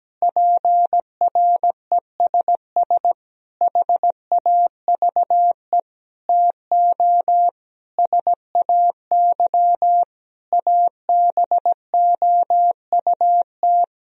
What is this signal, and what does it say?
Morse code, heard as a keyed tone, about 17 wpm, 700 Hz. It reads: PRESS HAVE TO SAY ABOUT
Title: Sentences from Top 1000 Words - Review Part 3 - 17wpm